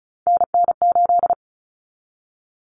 Morse code, heard as a keyed tone, about 34 wpm, 700 Hz. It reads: NN8
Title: Common Cut Numbers - 3-Digit 35wpm